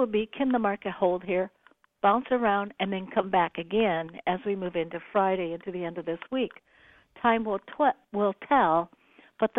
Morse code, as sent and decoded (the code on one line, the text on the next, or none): none